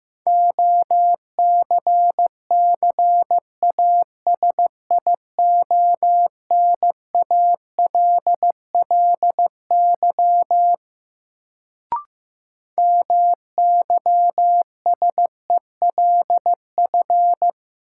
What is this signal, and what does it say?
Morse code, heard as a keyed tone, about 15 wpm, 700 Hz. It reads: OCCASIONALLY MYSELF
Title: Sentences from Top 1300 Words - New Word Drill Alt - 15wpm